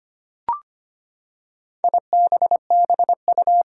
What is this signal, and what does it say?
Morse code, heard as a keyed tone, about 25 wpm, 700 Hz. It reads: IBBU
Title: Four Challenging Letter-Number Combinations 25wpm